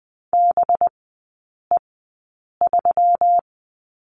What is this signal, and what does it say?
Morse code, heard as a keyed tone, about 20 wpm, spaced as for 8 wpm, 700 Hz. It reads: BE3
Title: 184 Lesson 30e - Letter b - Three Characters 20wpm